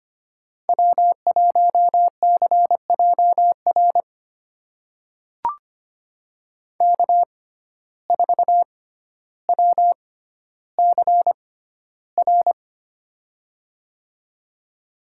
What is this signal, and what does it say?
Morse code, heard as a keyed tone, about 25 wpm, spaced as for 13 wpm, 700 Hz. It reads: W1CJR K4WCR
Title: USA Call Signs - 5 Characters - 6x Character Spacing - 25wpm